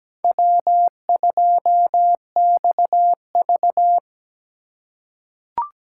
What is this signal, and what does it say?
Morse code, heard as a keyed tone, about 17 wpm, 700 Hz. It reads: W2XV
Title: USA Call Signs - 4 Characters - 4x Character Spacing - 17wpm